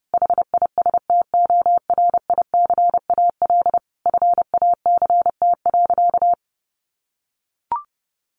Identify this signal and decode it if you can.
Morse code, heard as a keyed tone, about 30 wpm, 700 Hz. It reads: HISTORICAL FACT.
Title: Sentences from Top 600 Words 30wpm